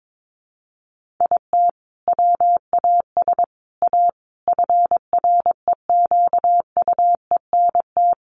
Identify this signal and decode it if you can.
Morse code, heard as a keyed tone, about 22 wpm, 700 Hz. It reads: IT WAS A FREQUENT
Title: Sentences from Top 1400 Words 22wpm